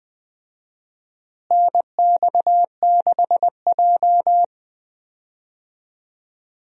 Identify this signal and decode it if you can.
Morse code, heard as a keyed tone, about 20 wpm, 700 Hz. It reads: NX6J
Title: USA Call Signs - 4 Characters - Repeated 3x - 20wpm